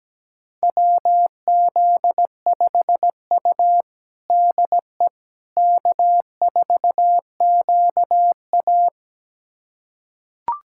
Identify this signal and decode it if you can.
Morse code, heard as a keyed tone, about 17 wpm, 700 Hz. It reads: WZ5U DE K4QA
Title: USA Callsign DE Callsign - 4 Characters - 17wpm